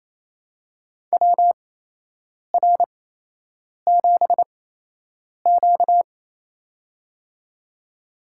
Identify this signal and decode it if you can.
Morse code, heard as a keyed tone, about 28 wpm, spaced as for 8 wpm, 700 Hz. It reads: WR7Q
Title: USA Call Signs - 4 Characters - 8x Character Spacing - 28wpm